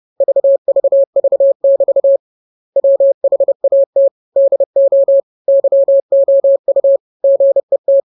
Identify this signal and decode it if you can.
Morse code, heard as a keyed tone, about 30 wpm, 550 Hz. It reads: VVV<BT> WHAT DO YOU GET